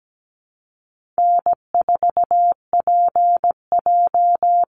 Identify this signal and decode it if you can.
Morse code, heard as a keyed tone, about 17 wpm, 700 Hz. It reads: N4PJ
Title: Easy Call Signs Repeated 3x - 17wpm